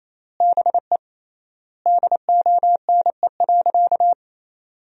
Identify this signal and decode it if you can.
Morse code, heard as a keyed tone, about 27 wpm, 700 Hz. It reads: BE DONE.
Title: 3x Word-Spacing - Sentences from Top 500 Words - 28wpm